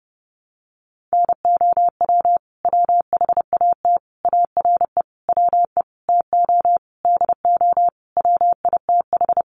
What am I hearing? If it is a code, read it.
Morse code, heard as a keyed tone, about 30 wpm, 700 Hz. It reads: NOW WHAT ARE WE TO DO WITH
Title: Sentences from Top 100 Words 30wpm